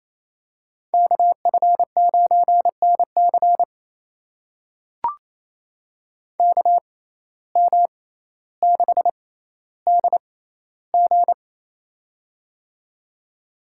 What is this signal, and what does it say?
Morse code, heard as a keyed tone, about 28 wpm, spaced as for 14 wpm, 700 Hz. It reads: KF9NC KM6DG
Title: USA Call Signs - 5 Characters - 6x Character Spacing - 28wpm